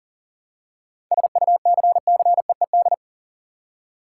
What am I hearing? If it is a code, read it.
Morse code, heard as a keyed tone, about 40 wpm, 700 Hz. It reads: SUCCEED
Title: Seven Letter Words - 40wpm